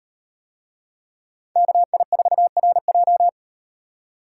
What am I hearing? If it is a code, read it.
Morse code, heard as a keyed tone, about 38 wpm, 700 Hz. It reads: KI4RJ